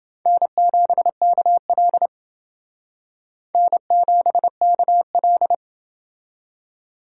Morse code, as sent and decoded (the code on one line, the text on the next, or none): -. --... -.- .-.. / -. --... -.- .-..
N7KL N7KL